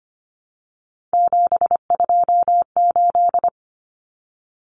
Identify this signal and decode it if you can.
Morse code, heard as a keyed tone, about 25 wpm, 700 Hz. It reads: 728
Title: Common Cut Numbers - 3-Digit 25wpm